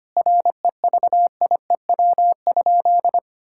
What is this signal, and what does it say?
Morse code, heard as a keyed tone, about 25 wpm, 700 Hz. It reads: REVIEW?
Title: Sentences from Top 1000 Words - Review Part 6 - 25wpm